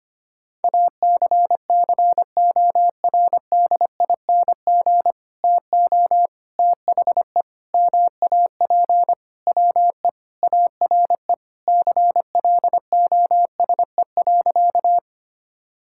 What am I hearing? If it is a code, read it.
Morse code, heard as a keyed tone, about 25 wpm, 700 Hz. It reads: ACCORDING TO THE MAP WE ARE CLOSE.
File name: practice / 25wpm / Sentences with 2-Word Phrases - 25wpm